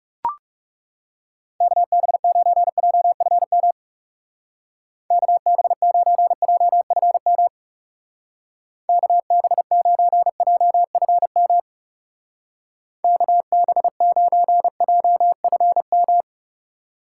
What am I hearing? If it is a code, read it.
Morse code, heard as a keyed tone, about 36 wpm, 700 Hz. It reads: KB9JFM KB9JFM KB9JFM KB9JFM
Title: Speed Racing - USA Call Signs 30wpm